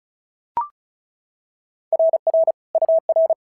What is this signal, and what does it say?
Morse code, heard as a keyed tone, about 34 wpm, 650 Hz. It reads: RR UR
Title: SOTA QSO Practice 35wpm